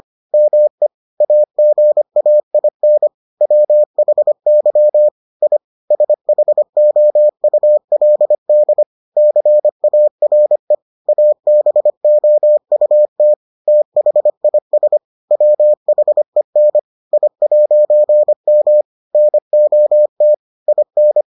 Morse code, heard as a keyed tone, about 25 wpm, 600 Hz. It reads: ME AGAIN WHY I SHOULD CARE ABOUT THIS WHEN I'M NOT IN